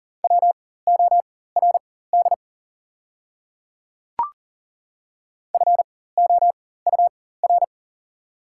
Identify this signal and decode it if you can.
Morse code, heard as a keyed tone, about 40 wpm, spaced as for 19 wpm, 690 Hz. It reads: WORD FOUR
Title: 4x Character Spacing - Top 300 Words - 40wpm